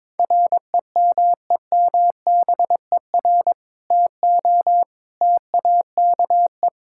Morse code, heard as a keyed tone, about 21 wpm, 700 Hz. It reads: REMEMBER TO TAKE
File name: practice / 22wpm / Sentences with 2-Word Phrases - 22wpm